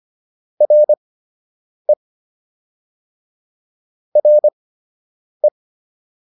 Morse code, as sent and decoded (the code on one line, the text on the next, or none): .-. / .-. .
RE RE